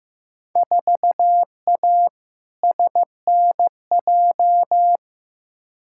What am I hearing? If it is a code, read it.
Morse code, heard as a keyed tone, about 15 wpm, 700 Hz. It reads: 4A SNJ
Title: Field Day Exchanges Encore - 15wpm